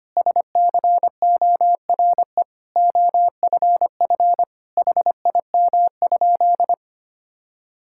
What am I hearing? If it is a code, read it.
Morse code, heard as a keyed tone, about 25 wpm, 700 Hz. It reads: SCORE OFF HIM?
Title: Sentences from Top 1600 Words 25wpm